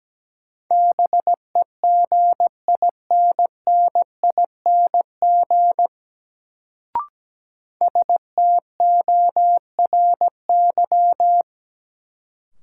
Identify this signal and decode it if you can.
Morse code, heard as a keyed tone, about 17 wpm, 700 Hz. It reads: BEGINNING STORY
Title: Sets of 3 Words - New Word Drill - 17wpm